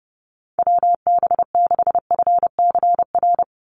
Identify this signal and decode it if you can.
Morse code, heard as a keyed tone, about 30 wpm, 700 Hz. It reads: WB6FCR
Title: USA Call Signs 30wpm